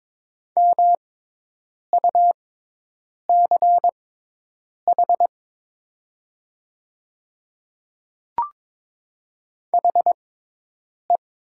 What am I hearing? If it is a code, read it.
Morse code, heard as a keyed tone, about 22 wpm, spaced as for 8 wpm, 700 Hz. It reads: MUCH HE
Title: 6x Character Spacing - Top 200 Words - 22wpm